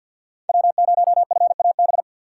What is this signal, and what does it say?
Morse code, heard as a keyed tone, about 52 wpm, 700 Hz. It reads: W0FAB
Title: USA Call Signs - 6x Character Spacing - 50wpm